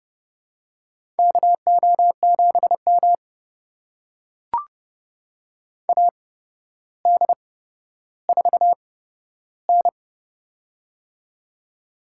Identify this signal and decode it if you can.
Morse code, heard as a keyed tone, about 30 wpm, spaced as for 13 wpm, 700 Hz. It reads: KO7M AD4N
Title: USA Call Signs - 4 Characters - 8x Character Spacing - 30wpm